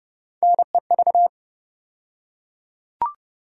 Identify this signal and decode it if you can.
Morse code, heard as a keyed tone, about 30 wpm, 700 Hz. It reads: NEV
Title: USA Call Sign Postfixes - 3 Letters 30wpm